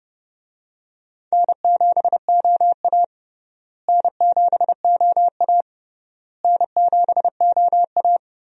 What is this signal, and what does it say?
Morse code, heard as a keyed tone, about 32 wpm, 700 Hz. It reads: N7OA N7OA N7OA
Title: USA Call Signs - 4 Characters - Repeated 3x - 30wpm